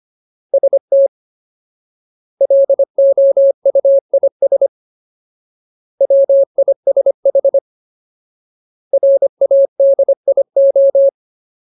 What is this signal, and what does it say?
Morse code, heard as a keyed tone, about 25 wpm, 550 Hz. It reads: ST LOUIS WISH RADIO
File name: Ultimate Word List / 25WPM Ultimate